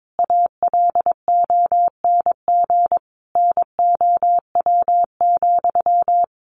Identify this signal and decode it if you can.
Morse code, heard as a keyed tone, about 22 wpm, 700 Hz. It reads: ALONG NOW,
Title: Sentences from Top 700 Words 22wpm